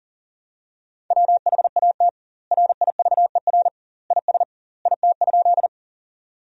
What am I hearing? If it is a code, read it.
Morse code, heard as a keyed tone, about 40 wpm, 700 Hz. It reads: WHAT RIVER IS IT?
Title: 2x Word-Spacing - Sentences from Top 300 Words - 40wpm